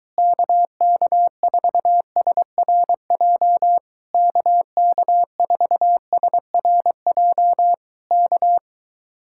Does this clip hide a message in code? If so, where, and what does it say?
Yes: Morse code from 0.2 to 8.6 s, heard as a keyed tone, about 23 wpm, 700 Hz. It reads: KK4SRJ KK4SRJ K